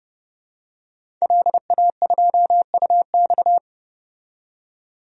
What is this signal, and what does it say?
Morse code, heard as a keyed tone, about 30 wpm, 700 Hz. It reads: LA2UX
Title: DXCC-100 Call Signs - 5 Characters - 30wpm